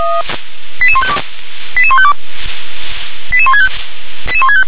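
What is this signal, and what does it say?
Five-tone selective call (ZVEI), heard as quick runs of five tones, tones of about 70 ms. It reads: 88152 88152 88165
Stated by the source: Received on an SDR.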